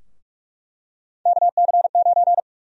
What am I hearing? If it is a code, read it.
Morse code, heard as a keyed tone, about 46 wpm, 700 Hz. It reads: KC9
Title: USA Call Sign Prefixes - Repeated 2x - 45wpm